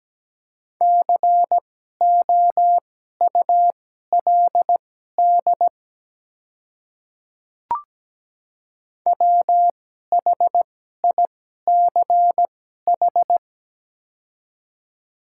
Morse code, heard as a keyed tone, about 17 wpm, spaced as for 12 wpm, 700 Hz. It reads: COULD WHICH
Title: 2x Character Spacing - Top 200 Words - 17wpm